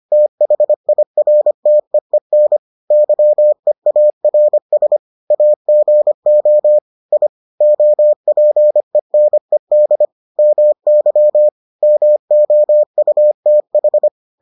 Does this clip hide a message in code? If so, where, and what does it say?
Yes: Morse code from 0.1 to 14.1 s, heard as a keyed tone, about 25 wpm, 600 Hz. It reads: THIRTEEN YEARS AGO I OPENED MY MOUTH